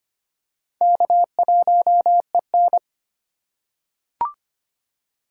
Morse code, heard as a keyed tone, about 25 wpm, 700 Hz. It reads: K1EN
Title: USA Call Signs - 4x Character Spacing - 25wpm